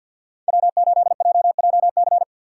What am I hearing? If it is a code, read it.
Morse code, heard as a keyed tone, about 53 wpm, 700 Hz. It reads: W8JJC
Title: USA Call Signs - 5 Characters - 4x Character Spacing - 50wpm